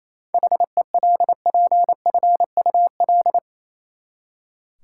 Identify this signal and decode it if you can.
Morse code, heard as a keyed tone, about 27 wpm, 700 Hz. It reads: HELPFUL
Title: Seven Letter Words - 28wpm